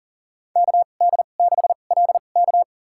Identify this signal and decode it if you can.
Morse code, heard as a keyed tone, about 40 wpm, spaced as for 29 wpm, 700 Hz. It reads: KD6LK
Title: USA Call Signs - 2x Character Spacing - 40wpm